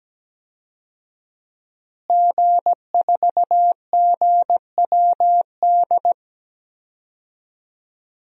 Morse code, heard as a keyed tone, about 17 wpm, 700 Hz. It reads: G4GWD
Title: DXCC-100 Call Signs - 5 Characters - Repeated 2x - 17wpm